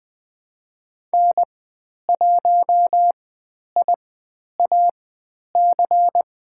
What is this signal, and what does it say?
Morse code, heard as a keyed tone, about 20 wpm, spaced as for 10 wpm, 700 Hz. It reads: N1IAC
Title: 102 Lesson 17g - Letter u - Call Signs 20wpm